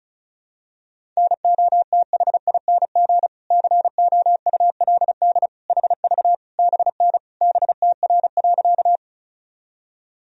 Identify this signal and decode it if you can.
Morse code, heard as a keyed tone, about 36 wpm, 700 Hz. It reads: NOTHING COULD HV BN BTR.
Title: Sentences from Top 500 Words and Abbreviations 35wpm